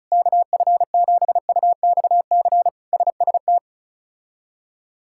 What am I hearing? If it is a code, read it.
Morse code, heard as a keyed tone, about 36 wpm, 700 Hz. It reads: KF7UXC SST
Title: K1USN SST QSO Practice - 35wpm